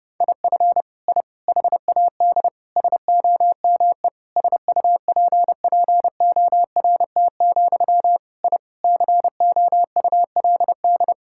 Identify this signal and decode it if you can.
Morse code, heard as a keyed tone, about 30 wpm, 700 Hz. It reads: IF I HAD SOME SUPPORT, I COULD
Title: Sentences from Top 1000 Words - Review Part 5 - 30wpm